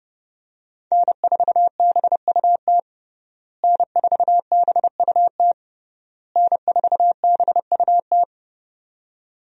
Morse code, heard as a keyed tone, about 30 wpm, 700 Hz. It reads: N4BUT N4BUT N4BUT
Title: USA Call Signs - 5 Characters - Repeated 3x - 30wpm